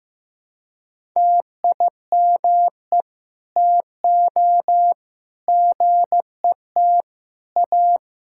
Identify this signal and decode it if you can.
Morse code, heard as a keyed tone, about 15 wpm, 700 Hz. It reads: TIME TO GET A
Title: Sentences from Top 1600 Words 15wpm